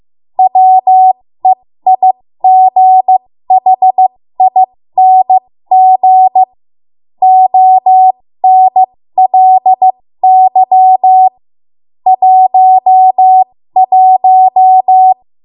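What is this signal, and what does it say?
Morse code, heard as a keyed tone, about 15 wpm, spaced as for 13 wpm, 750 Hz. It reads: WEIGHING ONLY 11
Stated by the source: 13 wpm Morse code practice archive transmission.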